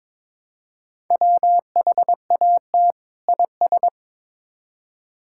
Morse code, heard as a keyed tone, about 21 wpm, 700 Hz. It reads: WHAT IS
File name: Build A Phrase - 2-Word Phrases - 22wpm